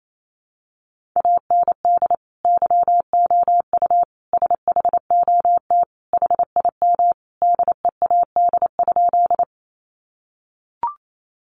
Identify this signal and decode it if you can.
Morse code, heard as a keyed tone, about 28 wpm, 700 Hz. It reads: AND YOU SHOT HIM DEAD?
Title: Sentences from Top 1500 Words 28wpm